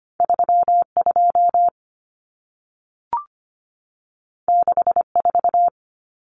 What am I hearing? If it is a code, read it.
Morse code, heard as a keyed tone, about 25 wpm, 700 Hz. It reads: 32 64
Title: Numbers - 2-Digit 25wpm